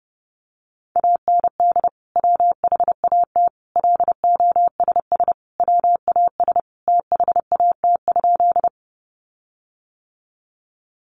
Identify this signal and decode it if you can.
Morse code, heard as a keyed tone, about 30 wpm, 700 Hz. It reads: AND WHAT LOSS WAS THAT?
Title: Sentences from Top 1400 Words 30wpm